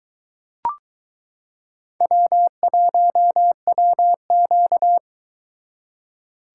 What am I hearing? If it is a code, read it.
Morse code, heard as a keyed tone, about 22 wpm, 700 Hz. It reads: W1WQ